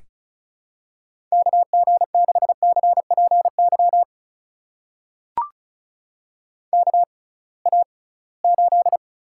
Morse code, heard as a keyed tone, about 34 wpm, spaced as for 22 wpm, 700 Hz. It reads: KG6CPY KA8
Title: USA Call Signs - 6 Characters - 6x Character Spacing - 35wpm